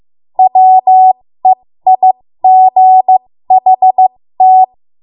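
Morse code, heard as a keyed tone, about 15 wpm, spaced as for 13 wpm, 750 Hz. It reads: WEIGHT
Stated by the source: ARRL Morse code practice at 13 WPM.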